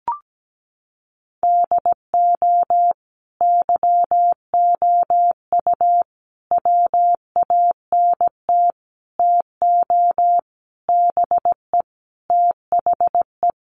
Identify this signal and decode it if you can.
Morse code, heard as a keyed tone, about 17 wpm, 700 Hz. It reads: DO YOU WANT TO BE THE
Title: Sentences from Top 1100 Words 17wpm